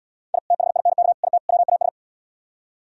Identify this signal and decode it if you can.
Morse code, heard as a keyed tone, about 77 wpm, 700 Hz. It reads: I THINK IT WAS